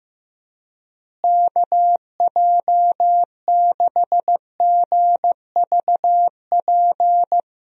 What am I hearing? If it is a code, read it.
Morse code, heard as a keyed tone, about 15 wpm, 700 Hz. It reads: KJ6GVP
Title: USA Call Signs - 6 Characters - Repeated 3x - 15wpm